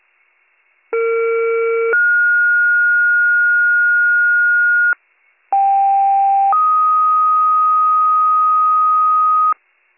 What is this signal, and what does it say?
Two-tone page, heard as a one-second tone, then a three-second tone, about 460 then 1430 Hz, about 790 then 1250 Hz, with noise. It is heard from 0.9 to 9.5 s.